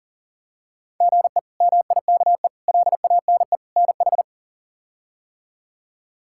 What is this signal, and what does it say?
Morse code, heard as a keyed tone, about 40 wpm, 700 Hz. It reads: GE MIKE LANE NH
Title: K1USN SST QSO Practice - 40wpm